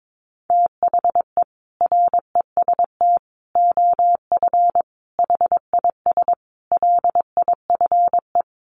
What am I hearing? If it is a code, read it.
Morse code, heard as a keyed tone, about 22 wpm, 700 Hz. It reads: THE REST OF HIS LIFE
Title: Sets of 5 Words 22wpm